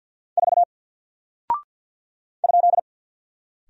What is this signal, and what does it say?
Morse code, heard as a keyed tone, about 50 wpm, 700 Hz. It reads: 4 ?
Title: Single Character (Letter:Number:Punctuation:Prosign) 50wpm